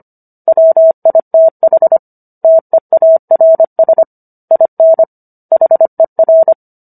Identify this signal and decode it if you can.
Morse code, heard as a keyed tone, about 25 wpm, 650 Hz. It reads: WITH TEARS IN HER